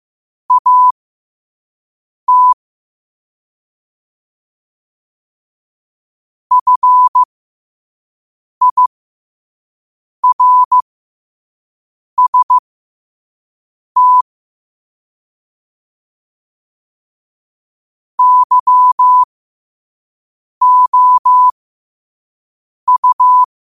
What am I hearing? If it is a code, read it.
Morse code, heard as a keyed tone, about 15 wpm, spaced as for 5 wpm, 1000 Hz. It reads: AT FIRST YOU
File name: qotdfiles / 5wpm